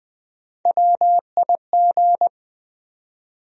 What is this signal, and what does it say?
Morse code, heard as a keyed tone, about 20 wpm, 700 Hz. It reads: WIG
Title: USA Call Sign Postfixes - 3 Letters 20wpm